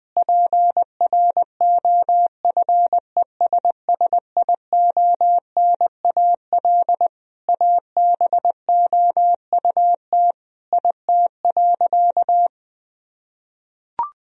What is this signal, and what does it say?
Morse code, heard as a keyed tone, about 21 wpm, 700 Hz. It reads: PROFESSIONAL ABOUT IT.
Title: Sentences from Top 1000 Words - Review Part 4 - 20wpm